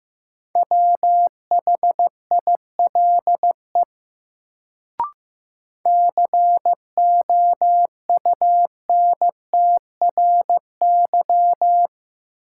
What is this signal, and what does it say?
Morse code, heard as a keyed tone, about 15 wpm, 700 Hz. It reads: WHILE COUNTRY
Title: Sentences from Top 300 Words - New Word Drill - 15wpm